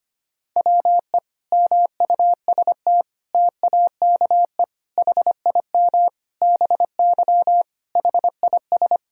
Morse code, heard as a keyed tone, about 25 wpm, 700 Hz. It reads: WE MUST TAKE HIM BY HIS
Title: Sentences from Top 1000 Words - Review Part 1 - 25wpm